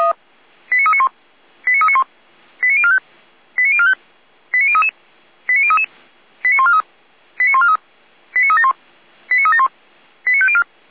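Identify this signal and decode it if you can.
Five-tone selective call (ZVEI), heard as quick runs of five tones, tones of about 70 ms. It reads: 89391 89391 89045 89045 89033 89033 89143 89143 89371 89371 89504
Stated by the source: ZVEI five-tone selective call received by SDR.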